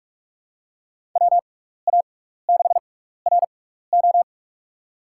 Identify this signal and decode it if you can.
Morse code, heard as a keyed tone, about 42 wpm, spaced as for 15 wpm, 700 Hz. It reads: WA6RO
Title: USA Call Signs - 5 Characters - 6x Character Spacing - 45wpm